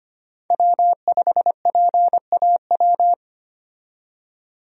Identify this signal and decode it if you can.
Morse code, heard as a keyed tone, about 25 wpm, 700 Hz. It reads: W5PAW
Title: USA Call Signs - 5 Characters - 25wpm